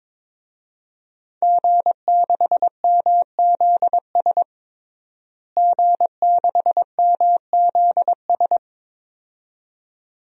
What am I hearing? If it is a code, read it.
Morse code, heard as a keyed tone, about 21 wpm, 700 Hz. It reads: G6MZS G6MZS
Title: DXCC-100 Call Signs - 5 Characters - Repeated 2x - 22wpm